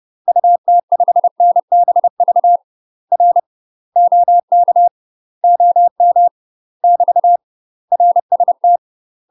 Morse code, heard as a keyed tone, about 30 wpm, 700 Hz. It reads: UT5NBV R OK OM <BT> RST
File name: UK MASTER QSOs TRAINING / 30WPM / UK MASTER QSOs 0036